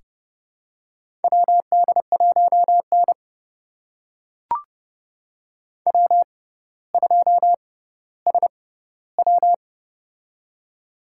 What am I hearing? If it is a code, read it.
Morse code, heard as a keyed tone, about 30 wpm, spaced as for 15 wpm, 700 Hz. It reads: WD1N W2SW
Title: USA Call Signs - 4 Characters - 6x Character Spacing - 30wpm